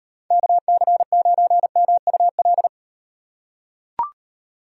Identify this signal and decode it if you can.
Morse code, heard as a keyed tone, about 38 wpm, 700 Hz. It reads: KC9MUL